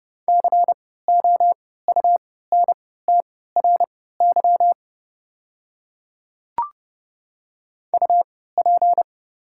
Morse code, heard as a keyed tone, about 30 wpm, spaced as for 17 wpm, 700 Hz. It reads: COUNTRY UP
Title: 3x Character Spacing - Top 500 Words - 30wpm